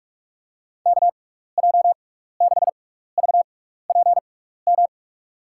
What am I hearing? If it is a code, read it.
Morse code, heard as a keyed tone, about 42 wpm, spaced as for 15 wpm, 700 Hz. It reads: KJ6VPM
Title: USA Call Signs - 6 Characters - 6x Character Spacing - 45wpm